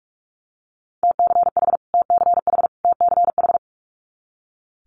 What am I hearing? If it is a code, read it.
Morse code, heard as a keyed tone, about 46 wpm, 700 Hz. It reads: TCH TCH TCH
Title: Multi-letter Phonemes Repeated 3x 45wpm